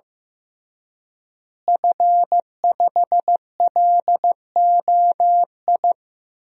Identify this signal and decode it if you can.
Morse code, heard as a keyed tone, about 15 wpm, 700 Hz. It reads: F5LOI
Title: DXCC-100 Call Signs - 5 Characters - Repeated 3x - 15wpm